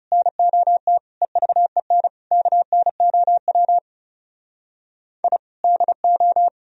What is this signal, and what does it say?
Morse code, heard as a keyed tone, about 37 wpm, 700 Hz. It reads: NOT EVEN KNOW I DO
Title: Speed-Racing - Sets of 5 Words 30wpm